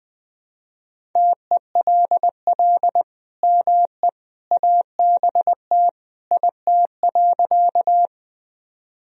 Morse code, heard as a keyed tone, about 21 wpm, 700 Hz. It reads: TELL ME ABT IT.
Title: Sentences from Top 200 Words and Abbreviations 20wpm